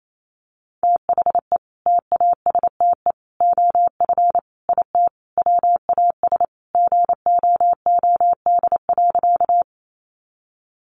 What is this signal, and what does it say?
Morse code, heard as a keyed tone, about 28 wpm, 700 Hz. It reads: THE TASTE OF IT WAS GOOD.
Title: Sentences from Top 1100 Words 28wpm